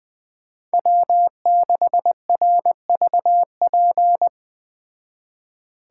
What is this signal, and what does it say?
Morse code, heard as a keyed tone, about 20 wpm, 700 Hz. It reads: W6RVP